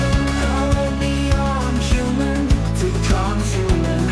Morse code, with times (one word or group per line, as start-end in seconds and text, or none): none